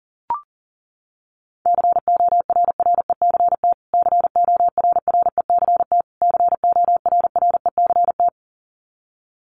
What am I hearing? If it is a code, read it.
Morse code, heard as a keyed tone, about 40 wpm, 700 Hz. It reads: CORRECT CORRECT CORRECT
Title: Top 500 Words Repeated 40wpm